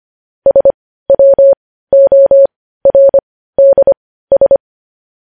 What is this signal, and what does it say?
Morse code, heard as a keyed tone, about 25 wpm, spaced as for 15 wpm, 550 Hz. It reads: SWORDS
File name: podcast / News headlines-1x-25x15